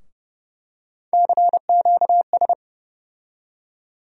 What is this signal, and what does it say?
Morse code, heard as a keyed tone, about 30 wpm, 700 Hz. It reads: CQS
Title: USA Call Sign Postfixes - 3 Letters 30wpm